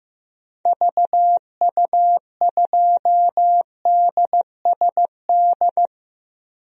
Morse code, heard as a keyed tone, about 15 wpm, 700 Hz. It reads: VU2DSD
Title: DXCC-100 Call Signs - 6 Characters - Repeated 2x - 15wpm